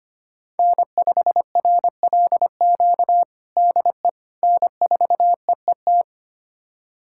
Morse code, heard as a keyed tone, about 25 wpm, 700 Hz. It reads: N5RLQ DE N4EET
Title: USA Callsign DE Callsign - 5 Characters - 25wpm